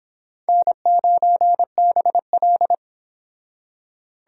Morse code, heard as a keyed tone, about 25 wpm, 700 Hz. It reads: N9BL